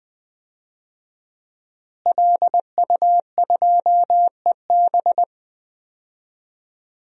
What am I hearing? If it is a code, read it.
Morse code, heard as a keyed tone, about 21 wpm, 700 Hz. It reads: LU2EB
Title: DXCC-100 Call Signs - 5 Characters - Repeated 3x - 20wpm